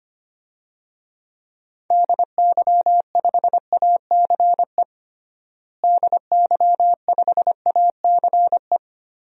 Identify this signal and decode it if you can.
Morse code, heard as a keyed tone, about 25 wpm, 700 Hz. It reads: DY5ACE DY5ACE
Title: DXCC-100 Call Signs - 6 Characters - Repeated 2x - 25wpm